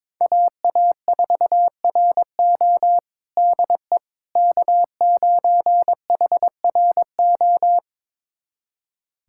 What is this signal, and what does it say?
Morse code, heard as a keyed tone, about 22 wpm, 700 Hz. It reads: AA4RO DE K9HRO
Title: USA Callsign DE Callsign - 5 Characters - 22wpm